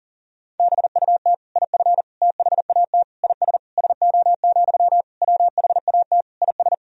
Morse code, heard as a keyed tone, about 40 wpm, 700 Hz. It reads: BUT IF THAT IS SO, WHAT IS
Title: Sentences from Top 200 Words and Abbreviations 40wpm